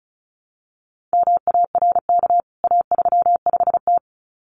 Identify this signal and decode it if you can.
Morse code, heard as a keyed tone, about 34 wpm, 700 Hz. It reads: MARK A35T
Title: CWT Exchange 35wpm